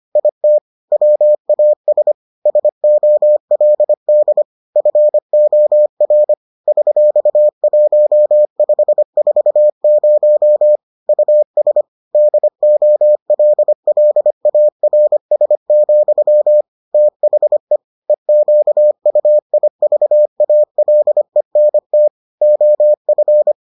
Morse code, heard as a keyed tone, about 25 wpm, 600 Hz. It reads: IT WAS SOLD FOR $1540 US DOLLARS, THE EQUIVALENT OF